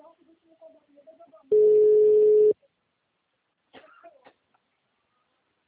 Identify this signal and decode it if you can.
Ringback tone (Europe, 425 Hz).
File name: out- -2012